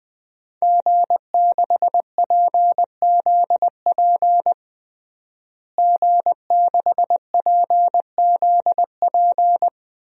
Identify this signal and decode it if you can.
Morse code, heard as a keyed tone, about 20 wpm, 700 Hz. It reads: G6PZP G6PZP
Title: DXCC-100 Call Signs - 5 Characters - Repeated 2x - 20wpm